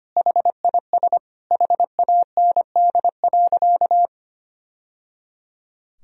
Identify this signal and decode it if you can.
Morse code, heard as a keyed tone, about 25 wpm, 700 Hz. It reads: HIS HAND.
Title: Sentences from Top 1000 Words - Review Part 1 - 25wpm